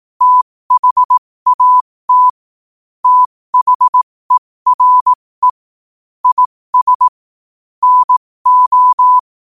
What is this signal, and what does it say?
Morse code, heard as a keyed tone, about 18 wpm, spaced as for 15 wpm, 1000 Hz. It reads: THAT THERE IS NO
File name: qotdfiles / 15wpm